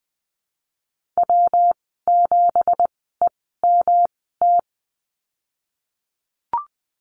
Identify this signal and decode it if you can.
Morse code, heard as a keyed tone, about 20 wpm, spaced as for 14 wpm, 700 Hz. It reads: W7EMT
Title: 179 Lesson 29g - Slash - Call Signs 20wpm